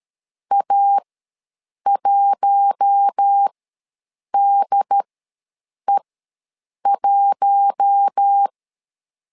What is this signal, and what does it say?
Morse code, heard as a keyed tone, about 13 wpm, spaced as for 7 wpm, 800 Hz. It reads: A1DE1
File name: LEZIONI / 4 Q1 ws 12wpm 2129